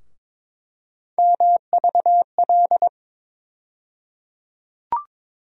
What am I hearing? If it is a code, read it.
Morse code, heard as a keyed tone, about 22 wpm, 700 Hz. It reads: MVL
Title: USA Call Sign Postfixes - 3 Letters 22wpm